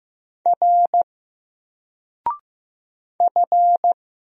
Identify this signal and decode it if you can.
Morse code, heard as a keyed tone, about 15 wpm, 700 Hz. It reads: R F
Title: Single Character (Letter:Number:Punctuation:Prosign) 15wpm